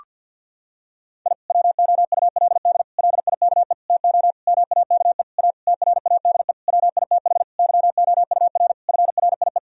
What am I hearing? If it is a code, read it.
Morse code, heard as a keyed tone, about 50 wpm, 700 Hz. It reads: I WOULD LIKE TO MAKE A TRADE WITH YOUR FRIE
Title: Sentences from Top 1000 Words - Review Part 4 - 50wpm